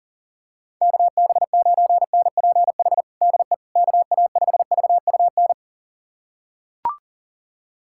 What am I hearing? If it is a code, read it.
Morse code, heard as a keyed tone, about 40 wpm, 700 Hz. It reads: KB9NPH DE KA5VUN
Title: USA Callsign DE Callsign 40wpm